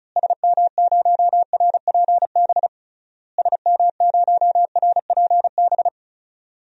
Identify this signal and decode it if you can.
Morse code, heard as a keyed tone, about 36 wpm, 700 Hz. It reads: SM0RPB SM0RPB
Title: DXCC-100 Call Signs - 6 Characters - Repeated 3x - 35wpm